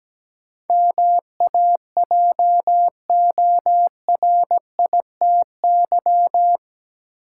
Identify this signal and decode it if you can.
Morse code, heard as a keyed tone, about 17 wpm, 700 Hz. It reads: MAJORITY
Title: Sets of 3 Words - New Word Drill - 17wpm